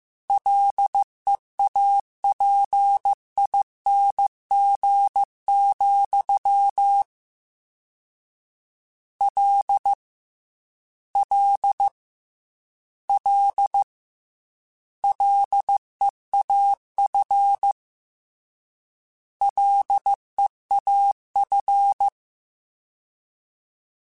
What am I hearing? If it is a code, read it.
Morse code, heard as a keyed tone, about 15 wpm, 800 Hz. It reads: LEAPING, L L L LEAF LEAF
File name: Combinations / Combinations-mp3s / Combinations@13